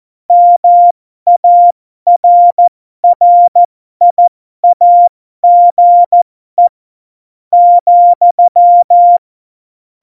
Morse code, heard as a keyed tone, about 14 wpm, spaced as for 12 wpm, 700 Hz. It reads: MARRIAGE ,
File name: Morsekurs / Witze